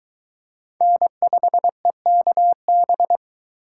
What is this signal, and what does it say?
Morse code, heard as a keyed tone, about 22 wpm, 700 Hz. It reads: N5EKB